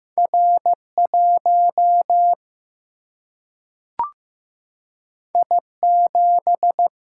Morse code, heard as a keyed tone, about 15 wpm, 700 Hz. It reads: R 1 I 7
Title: Single-Letter Single-Number 15wpm